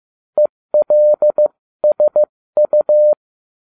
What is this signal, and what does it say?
Morse code, heard as a keyed tone, about 15 wpm, spaced as for 13 wpm, 600 Hz. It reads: ELSU